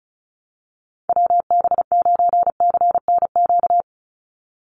Morse code, heard as a keyed tone, about 36 wpm, 700 Hz. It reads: WB9CNQ
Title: USA Call Signs 35wpm